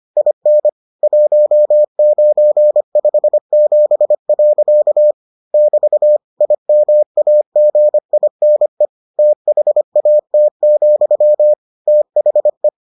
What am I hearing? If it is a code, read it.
Morse code, heard as a keyed tone, about 25 wpm, 600 Hz. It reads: IN 1957. <BT> IMAGINE THAT, THE